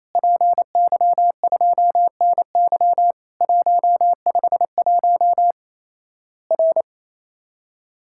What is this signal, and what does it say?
Morse code, heard as a keyed tone, about 28 wpm, 700 Hz. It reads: PY2NY 151 R
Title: Worked All Europe DX Contest - QTC Exchange - 28wpm